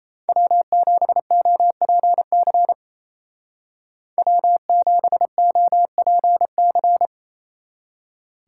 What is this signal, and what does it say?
Morse code, heard as a keyed tone, about 30 wpm, 700 Hz. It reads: W7OPC W7OPC
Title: Speed Racing - USA 1x3 Call Signs 28wpm